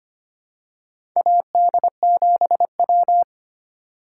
Morse code, heard as a keyed tone, about 25 wpm, 700 Hz. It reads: AD7W
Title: USA Call Signs - 4 Characters - 4x Character Spacing - 25wpm